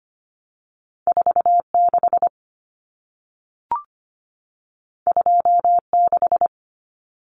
Morse code, heard as a keyed tone, about 25 wpm, 700 Hz. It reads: 46 26
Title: Common Cut Numbers - 2-Digit 25wpm